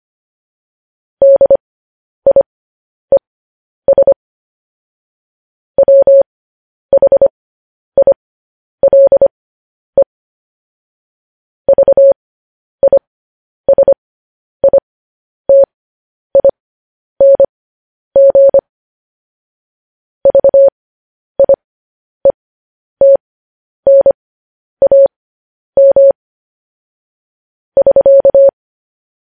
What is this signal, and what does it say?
Morse code, heard as a keyed tone, about 25 wpm, spaced as for 10 wpm, 550 Hz. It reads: DIES WHILE VISITING VIETNAM <SK>